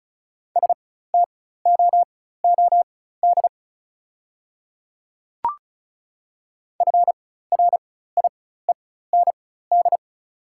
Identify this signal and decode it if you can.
Morse code, heard as a keyed tone, about 36 wpm, spaced as for 16 wpm, 700 Hz. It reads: STOOD FRIEND
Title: 4x Character Spacing - Top 500 Words - 35wpm